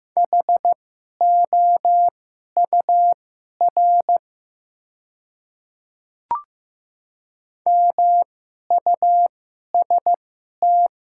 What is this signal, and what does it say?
Morse code, heard as a keyed tone, about 15 wpm, spaced as for 11 wpm, 700 Hz. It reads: HOUR MUST
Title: 2x Character Spacing - Top 500 Words - 15wpm